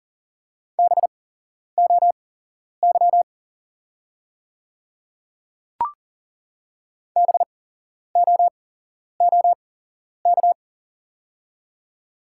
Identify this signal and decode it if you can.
Morse code, heard as a keyed tone, about 40 wpm, spaced as for 11 wpm, 700 Hz. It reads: BOY BOOK
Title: 8x Character Spacing - Top 300 Words - 40wpm